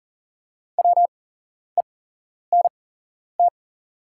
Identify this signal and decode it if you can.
Morse code, heard as a keyed tone, about 37 wpm, spaced as for 11 wpm, 690 Hz. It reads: WENT
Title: 8x Character Spacing - Top 200 Words - 40wpm